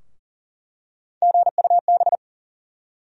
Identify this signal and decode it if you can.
Morse code, heard as a keyed tone, about 40 wpm, 700 Hz. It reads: GUB
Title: USA Call Sign Postfixes - 3 Letters 40wpm